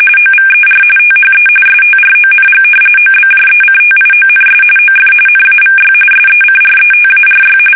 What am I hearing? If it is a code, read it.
RTTY, about 45.45 baud. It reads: OVER THE STRONG OBJECTIONS OF NAVY SECRETARY
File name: fsk850 / usnavy5